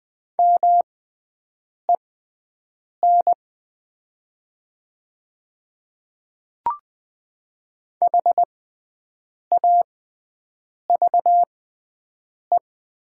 Morse code, heard as a keyed tone, about 20 wpm, spaced as for 7 wpm, 690 Hz. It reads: MEN HAVE
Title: 6x Character Spacing - Top 200 Words - 20wpm